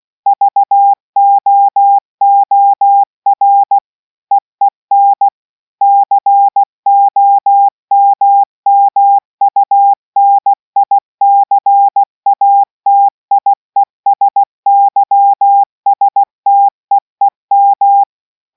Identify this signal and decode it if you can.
Morse code, heard as a keyed tone, about 16 wpm, 800 Hz. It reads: VOOR EEN COMMUNICATIESYSTEEM